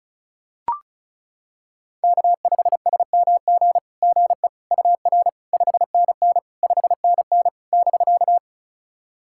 Morse code, heard as a keyed tone, about 35 wpm, 700 Hz. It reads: K5SMG GE UR 5NN 5NN <BK>
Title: SOTA QSO Practice 35wpm